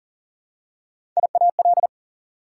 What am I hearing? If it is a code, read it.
Morse code, heard as a keyed tone, about 40 wpm, 700 Hz. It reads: IAL
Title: Prefixes and Suffixes 40wpm